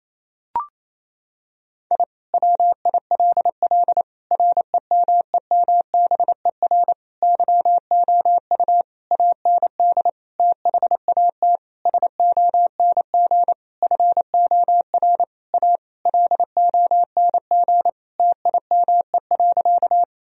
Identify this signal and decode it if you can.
Morse code, heard as a keyed tone, about 28 wpm, 700 Hz. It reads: I WILL REMEMBER YOU AND THAT SONG FOR A LONG TIME.
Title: Sentences from Top 1000 Words - Review Part 2 - 28wpm